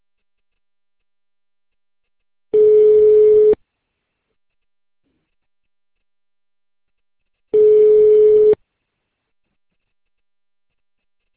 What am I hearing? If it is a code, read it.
Ringback tone (Europe, 425 Hz).